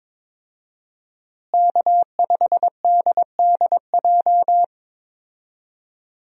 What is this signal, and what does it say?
Morse code, heard as a keyed tone, about 22 wpm, 700 Hz. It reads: K5DDJ
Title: USA Call Signs - 5 Characters - Repeated 2x - 22wpm